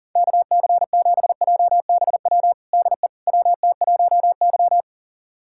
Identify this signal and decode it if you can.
Morse code, heard as a keyed tone, about 37 wpm, 700 Hz. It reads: KC7JBW DE WT1Y
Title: USA Callsign DE Callsign 40wpm